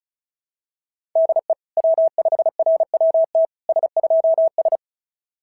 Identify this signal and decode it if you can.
Morse code, heard as a keyed tone, about 36 wpm, 650 Hz. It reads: DE W5RWT S2S
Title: SOTA QSO Practice 35wpm